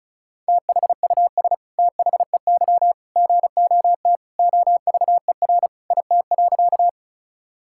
Morse code, heard as a keyed tone, about 36 wpm, 700 Hz. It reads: THUS THEY GOT OVER IT.
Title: Sentences from Top 1000 Words - Review Part 5 - 35wpm